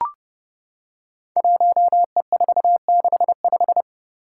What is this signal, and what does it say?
Morse code, heard as a keyed tone, about 30 wpm, 700 Hz. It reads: 1E465
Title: Code Groups - 5 Letter:Numbers - 30wpm